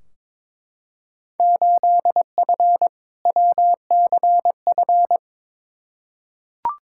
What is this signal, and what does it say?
Morse code, heard as a keyed tone, about 22 wpm, 700 Hz. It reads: 8F WCF
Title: Field Day Exchanges Encore - 22wpm